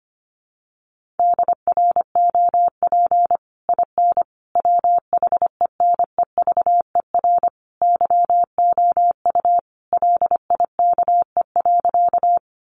Morse code, heard as a keyed tone, about 25 wpm, 700 Hz. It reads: DROP IN WHENEVER YOU LIKE.
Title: Sentences from Top 1400 Words 25wpm